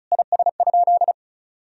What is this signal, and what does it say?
Morse code, heard as a keyed tone, about 36 wpm, 700 Hz. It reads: IS?
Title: Sentences from Top 1000 Words - Review Part 1 - 35wpm